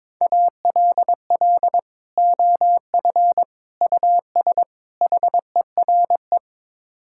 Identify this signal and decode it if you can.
Morse code, heard as a keyed tone, about 22 wpm, 700 Hz. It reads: ALL OF US HERE
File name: practice / 22wpm / Build A Phrase - 5-Word Phrases - 22wpm